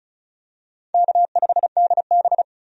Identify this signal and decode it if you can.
Morse code, heard as a keyed tone, about 34 wpm, 700 Hz. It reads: K5DB
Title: USA Call Signs - 4 Characters - 35wpm